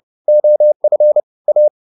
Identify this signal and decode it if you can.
Morse code, heard as a keyed tone, about 30 wpm, 600 Hz. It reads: OF A